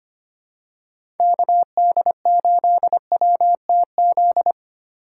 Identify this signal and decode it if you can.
Morse code, heard as a keyed tone, about 25 wpm, 700 Hz. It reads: KD8WTZ
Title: USA Call Signs - 2x Character Spacing - 25wpm